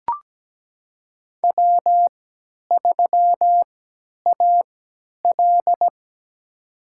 Morse code, heard as a keyed tone, about 17 wpm, spaced as for 10 wpm, 700 Hz. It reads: W3AL
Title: USA Call Signs - 4 Characters - 3x Character Spacing - 17wpm